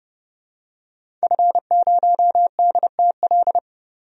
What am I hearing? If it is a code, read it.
Morse code, heard as a keyed tone, about 32 wpm, 700 Hz. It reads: F0DTL
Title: DXCC-100 Call Signs - 5 Characters - Repeated 3x - 30wpm